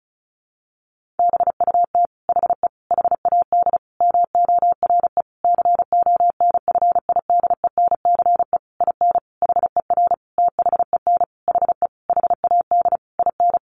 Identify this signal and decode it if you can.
Morse code, heard as a keyed tone, about 34 wpm, 700 Hz. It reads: BUT HE HAD MORE CONFIDENCE IN HER THEN HE HAD IN
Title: Sentences from Top 1500 Words 35wpm